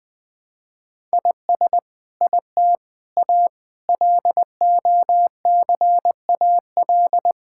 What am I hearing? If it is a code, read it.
Morse code, heard as a keyed tone, about 20 wpm, 700 Hz. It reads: IS IT A LOCAL
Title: Sentences from Top 1000 Words - Review Part 5 - 20wpm